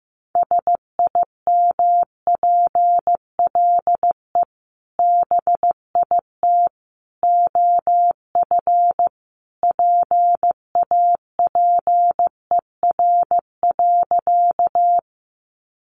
Morse code, heard as a keyed tone, about 15 wpm, 700 Hz. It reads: SIMPLE BIT OF PAPER.
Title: Sentences from Top 900 Words 15wpm